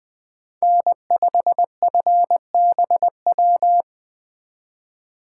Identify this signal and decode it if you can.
Morse code, heard as a keyed tone, about 24 wpm, 700 Hz. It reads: N5FBW
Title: Speed Racing - USA 1x3 Call Signs 20wpm